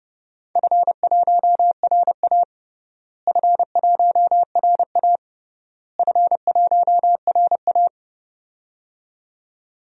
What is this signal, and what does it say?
Morse code, heard as a keyed tone, about 29 wpm, 700 Hz. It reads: F1RA F1RA F1RA
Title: DXCC-100 Call Signs - 4 Characters - Repeated 3x - 30wpm